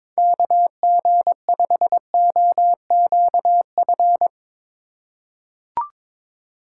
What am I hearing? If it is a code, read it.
Morse code, heard as a keyed tone, about 22 wpm, 700 Hz. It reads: KG5OQF
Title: USA Call Signs - 6 Characters - 4x Character Spacing - 22wpm